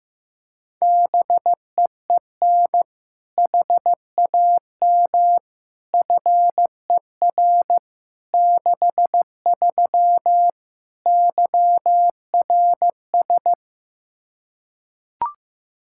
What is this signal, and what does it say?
Morse code, heard as a keyed tone, about 15 wpm, 700 Hz. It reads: BEEN HAM FER 63 YRS
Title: Practice QSO - Exchange 3 - More Info - 15wpm